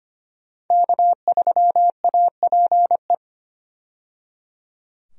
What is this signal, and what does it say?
Morse code, heard as a keyed tone, about 25 wpm, 700 Hz. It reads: K3APE
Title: USA Call Signs - 5 Characters - 25wpm